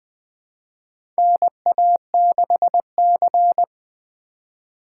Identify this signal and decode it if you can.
Morse code, heard as a keyed tone, about 20 wpm, 700 Hz. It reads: NA6C
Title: USA Call Signs - 4 Characters - 2x Character Spacing - 20wpm